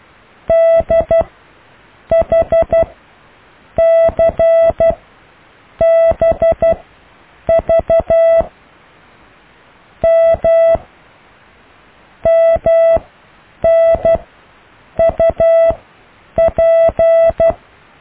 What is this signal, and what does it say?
Morse code, heard as a keyed tone, about 12 wpm, spaced as for 7 wpm, 660 Hz. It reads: DHCBV M MNUP
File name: cw / IJLUM+CDRO+EKTPV+BFHNW